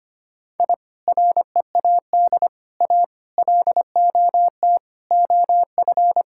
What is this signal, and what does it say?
Morse code, heard as a keyed tone, about 25 wpm, 700 Hz. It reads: I READ A LOT OF
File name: practice / 25wpm / Sentences with 4-Word Phrases - 25wpm